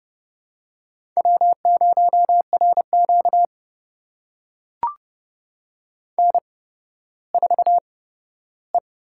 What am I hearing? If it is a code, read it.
Morse code, heard as a keyed tone, about 30 wpm, spaced as for 15 wpm, 700 Hz. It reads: W0RQ N4E
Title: USA Call Signs - 4 Characters - 8x Character Spacing - 30wpm